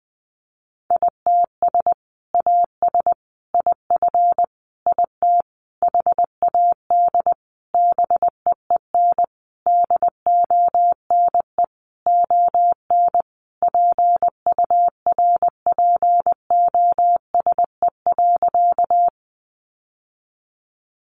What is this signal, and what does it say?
Morse code, heard as a keyed tone, about 20 wpm, 700 Hz. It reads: ITS AS IF IT HAD BEEN DONE ON PURPOSE.
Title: Sentences from Top 800 Words 20wpm